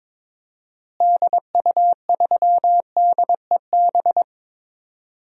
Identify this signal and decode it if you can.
Morse code, heard as a keyed tone, about 22 wpm, 700 Hz. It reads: DU3DEB
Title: DXCC-100 Call Signs - 6 Characters - Repeated 3x - 22wpm